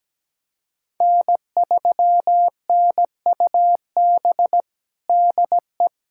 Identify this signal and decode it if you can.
Morse code, heard as a keyed tone, about 17 wpm, 700 Hz. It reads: N3NUB DE
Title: USA Callsign DE Callsign - 5 Characters - 17wpm